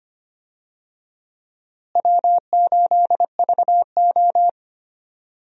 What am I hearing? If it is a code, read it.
Morse code, heard as a keyed tone, about 25 wpm, 700 Hz. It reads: W8VO
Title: USA Call Signs - 4 Characters - Repeated 2x - 25wpm